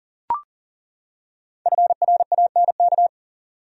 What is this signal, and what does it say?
Morse code, heard as a keyed tone, about 40 wpm, 700 Hz. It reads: FRANK
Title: USA Common Names - Max 5 Letters - 40wpm